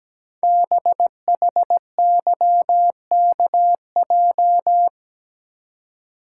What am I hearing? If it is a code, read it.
Morse code, heard as a keyed tone, about 17 wpm, 700 Hz. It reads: BHYKJ
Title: Code Groups - 5 Letters - 17wpm